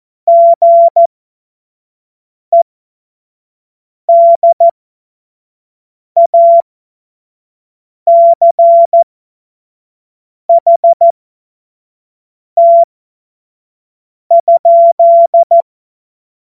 Morse code, heard as a keyed tone, about 14 wpm, spaced as for 5 wpm, 680 Hz. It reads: GEDACHT?
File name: Morsekurs / Witze / Witz 13 Februar 2024 25 F70